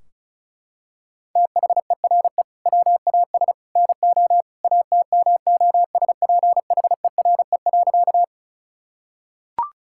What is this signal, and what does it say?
Morse code, heard as a keyed tone, about 36 wpm, 700 Hz. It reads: THERE WAS NO ATMOSPHERE.
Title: Sentences from Top 1600 Words 35wpm